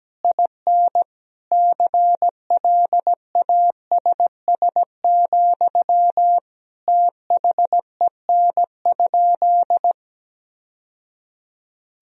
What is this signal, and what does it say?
Morse code, heard as a keyed tone, about 17 wpm, 700 Hz. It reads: IN CLASS, THEN?
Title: Sentences from Top 1000 Words - Review Part 4 - 17wpm